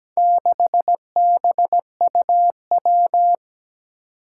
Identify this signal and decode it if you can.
Morse code, heard as a keyed tone, about 17 wpm, 700 Hz. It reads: 6BUW
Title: Four Challenging Letter-Number Combinations 17wpm